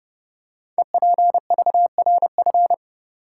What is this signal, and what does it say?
Morse code, heard as a keyed tone, about 30 wpm, 700 Hz. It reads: EPVRF
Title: Code Groups - 5 Letters - 30wpm